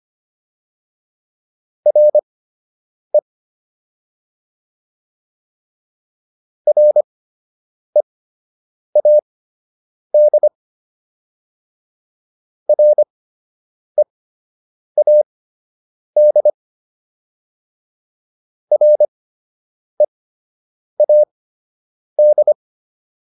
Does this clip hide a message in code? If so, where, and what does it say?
Yes: Morse code from 1.9 to 22.5 s, heard as a keyed tone, about 25 wpm, spaced as for 8 wpm, 600 Hz. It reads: RE READ READ READ